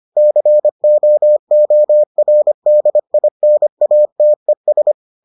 Morse code, heard as a keyed tone, about 25 wpm, 600 Hz. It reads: COORDINATES